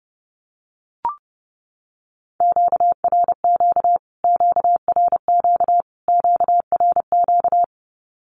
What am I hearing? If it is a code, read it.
Morse code, heard as a keyed tone, about 30 wpm, 700 Hz. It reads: QRQ QRQ QRQ
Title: Q-Codes Repeated 3x 30wpm